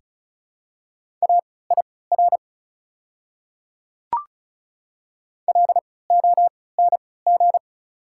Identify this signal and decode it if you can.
Morse code, heard as a keyed tone, about 36 wpm, spaced as for 20 wpm, 690 Hz. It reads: AIR LONG
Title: 3x Character Spacing - Top 500 Words - 35wpm